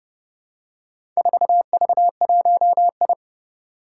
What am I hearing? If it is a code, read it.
Morse code, heard as a keyed tone, about 30 wpm, 690 Hz. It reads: 4V1I
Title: Four Challenging Letter-Number Combinations 30wpm